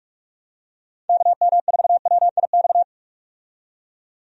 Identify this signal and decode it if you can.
Morse code, heard as a keyed tone, about 45 wpm, 700 Hz. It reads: KM4WIX
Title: USA Call Signs - 6 Characters - 45wpm